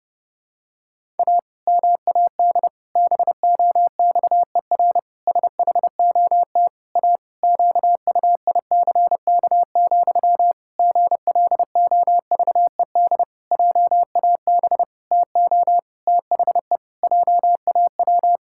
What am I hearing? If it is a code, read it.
Morse code, heard as a keyed tone, about 30 wpm, 700 Hz. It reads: A MAD BOXER SHOT A QUICK, GLOVED JAB TO THE JAW
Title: Pangram Challenge v2 - 30wpm